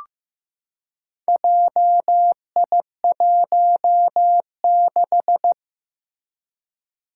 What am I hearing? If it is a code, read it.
Morse code, heard as a keyed tone, about 15 wpm, 700 Hz. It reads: JI16
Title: Four Challenging Letter-Number Combinations 15wpm